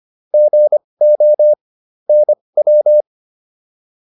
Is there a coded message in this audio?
Yes: Morse code from 0.3 to 3.0 s, heard as a keyed tone, about 25 wpm, spaced as for 20 wpm, 600 Hz. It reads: GO NW